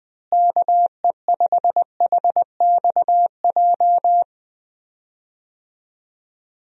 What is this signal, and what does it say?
Morse code, heard as a keyed tone, about 21 wpm, 700 Hz. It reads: KE5HXJ
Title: USA Call Signs - 6 Characters - Repeated 2x - 20wpm